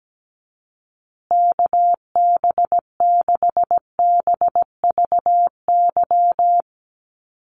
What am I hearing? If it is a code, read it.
Morse code, heard as a keyed tone, about 17 wpm, 700 Hz. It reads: KB6BVY
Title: USA Call Signs Repeated 2x 17wpm